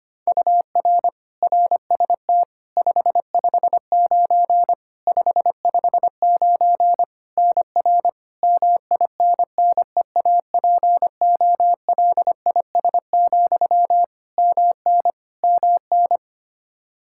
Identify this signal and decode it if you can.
Morse code, heard as a keyed tone, about 25 wpm, 700 Hz. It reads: UR RST 559 559 NR MINNEAPOLIS, MN MN
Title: Practice QSO - Exchange 1 - Beginning - 25wpm